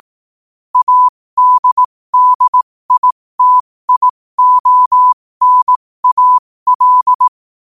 Morse code, heard as a keyed tone, about 18 wpm, spaced as for 15 wpm, 1000 Hz. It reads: ADDITIONAL
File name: qotdfiles / 15wpm